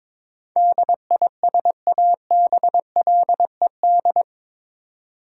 Morse code, heard as a keyed tone, about 22 wpm, 700 Hz. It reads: DISABLED
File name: practice / 22wpm / Eight Letter Words - 22wpm